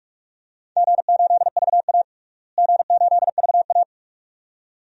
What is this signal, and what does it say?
Morse code, heard as a keyed tone, about 43 wpm, 700 Hz. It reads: G8VA G8VA
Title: DXCC-100 Call Signs - 4 Characters - Repeated 2x - 45wpm